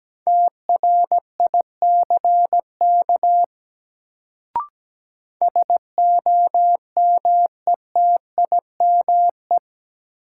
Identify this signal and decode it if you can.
Morse code, heard as a keyed tone, about 17 wpm, 700 Hz. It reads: TRICK SOMETIME
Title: Sentences from Top 1500 Words - New Word Drill Alt - 17wpm